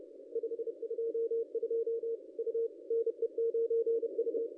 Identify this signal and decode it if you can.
Morse code, heard as a keyed tone, about 30 wpm, 410 Hz. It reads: 522 U NE9U
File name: contests / ss-cw-k5tr-op-n5rz -1100